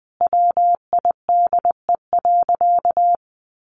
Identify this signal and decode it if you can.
Morse code, heard as a keyed tone, about 20 wpm, 700 Hz. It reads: WIDE.
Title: Sentences from Top 1200 Words 20wpm